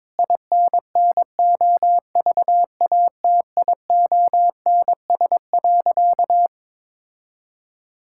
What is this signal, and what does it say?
Morse code, heard as a keyed tone, about 23 wpm, 700 Hz. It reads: INNOVATIONS.
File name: Sentences with 7-Word Phrases - 22wpm